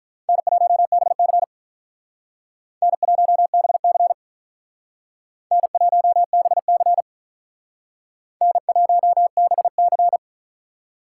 Morse code, heard as a keyed tone, about 43 wpm, 700 Hz. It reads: N1BC N1BC N1BC N1BC
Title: Speed Racing - USA 1x2 Call Signs 35wpm